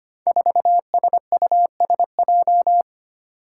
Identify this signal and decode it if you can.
Morse code, heard as a keyed tone, about 25 wpm, 700 Hz. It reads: 4SUSJ
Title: Five Challenging Letter-Number Combinations 25wpm